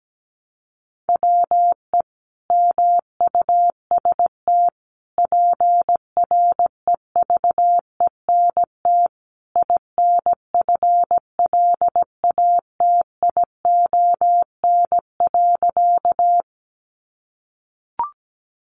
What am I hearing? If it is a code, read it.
Morse code, heard as a keyed tone, about 17 wpm, 700 Hz. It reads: WE MUST PREVENT INFLATION.
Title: Sentences from Top 1100 Words 17wpm